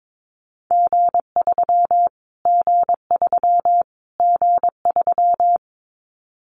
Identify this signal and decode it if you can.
Morse code, heard as a keyed tone, about 22 wpm, 700 Hz. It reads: G3 G3 G3
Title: Callsign Prefixes for Top 25 DXCC - Repeated 3x - 22wpm